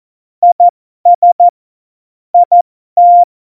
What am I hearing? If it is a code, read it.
Morse code, heard as a keyed tone, about 14 wpm, spaced as for 12 wpm, 700 Hz. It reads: IS IT
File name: Morsekurs / Witze